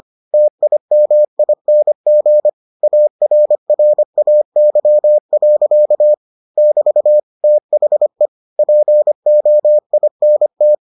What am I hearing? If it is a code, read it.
Morse code, heard as a keyed tone, about 25 wpm, 600 Hz. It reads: TIMING ARRAY. <BT> THE POINT